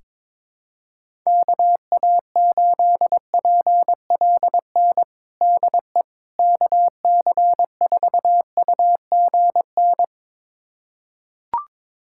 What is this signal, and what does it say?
Morse code, heard as a keyed tone, about 22 wpm, 700 Hz. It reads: KA8PLN DE KC4UGN
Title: USA Callsign DE Callsign 22wpm